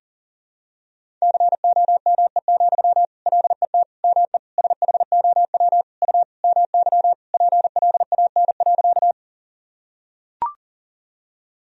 Morse code, heard as a keyed tone, about 40 wpm, 700 Hz. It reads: COME, LET ME SHOW U MY PLAN.
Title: Sentences from Top 500 Words and Abbreviations 40wpm